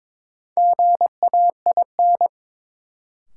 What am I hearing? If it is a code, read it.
Morse code, heard as a keyed tone, about 22 wpm, 700 Hz. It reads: GAIN
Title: Sentences from Top 1000 Words - New Word Drill - 22wpm